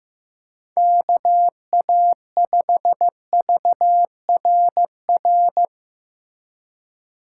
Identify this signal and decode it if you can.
Morse code, heard as a keyed tone, about 15 wpm, 700 Hz. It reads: KA5VRR
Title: USA Call Signs - 6 Characters - 2x Character Spacing - 15wpm